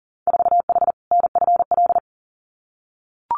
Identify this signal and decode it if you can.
Morse code, heard as a keyed tone, about 40 wpm, 700 Hz. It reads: 4H NFL
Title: Winter Field Day Exchanges Encore 40wpm